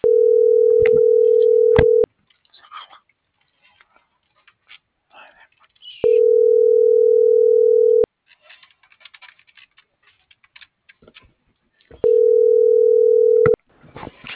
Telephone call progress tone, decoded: ringback tone

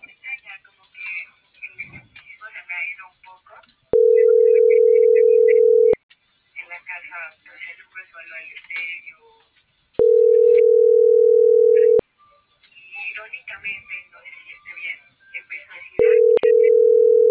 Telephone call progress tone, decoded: ringback tone

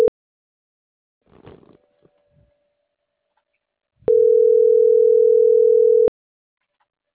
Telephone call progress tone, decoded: ringback tone

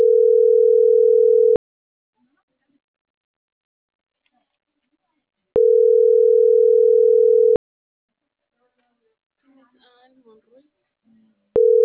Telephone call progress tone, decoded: ringback tone

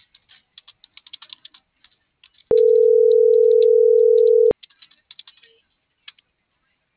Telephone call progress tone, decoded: ringback tone

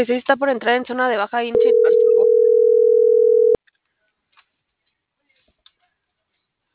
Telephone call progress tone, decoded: ringback tone

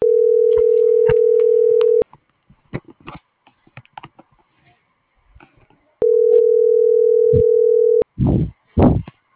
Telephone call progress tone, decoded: ringback tone

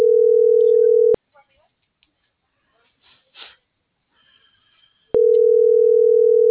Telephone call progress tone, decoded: ringback tone

